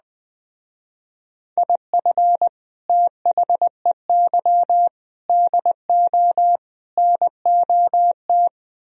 Morse code, heard as a keyed tone, about 20 wpm, 700 Hz. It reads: IF THEY DO NOT